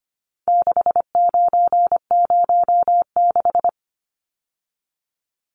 Morse code, heard as a keyed tone, about 25 wpm, 700 Hz. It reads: 6906
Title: Numbers - 4-Digit 25wpm